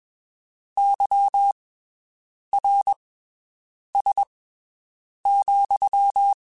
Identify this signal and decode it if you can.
Morse code, heard as a keyed tone, about 21 wpm, spaced as for 7 wpm, 800 Hz. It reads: YRS,
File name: (spaced) / QSOs Spaced-mp3s / WD0BGP 1 @20 sp